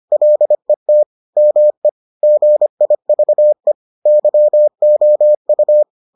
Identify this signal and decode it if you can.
Morse code, heard as a keyed tone, about 25 wpm, 600 Hz. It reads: LET ME GIVE YOU